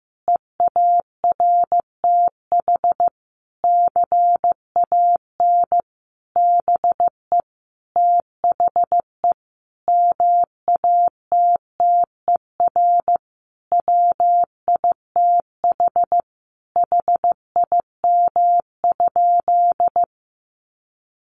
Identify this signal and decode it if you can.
Morse code, heard as a keyed tone, about 15 wpm, 700 Hz. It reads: EARTH CAN BE THE MATTER WITH HIM?
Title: Sentences from Top 700 Words 15wpm